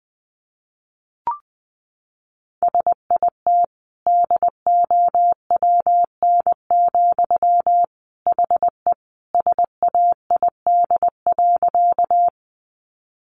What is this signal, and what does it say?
Morse code, heard as a keyed tone, about 20 wpm, 700 Hz. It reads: SIT DOWN, HE SAID.
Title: Sentences from Top 500 Words 20wpm